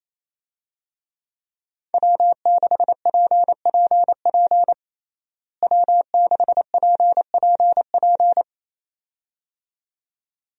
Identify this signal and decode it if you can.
Morse code, heard as a keyed tone, about 28 wpm, 700 Hz. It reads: W6PPP W6PPP
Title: USA Call Signs - 5 Characters - Repeated 2x - 28wpm